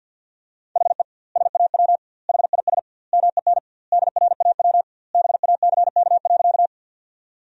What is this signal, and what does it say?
Morse code, heard as a keyed tone, about 50 wpm, 700 Hz. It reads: HE SAW HIS MEN DRAW BACK.
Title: 2x Word-Spacing - Sentences from Top 300 Words - 50wpm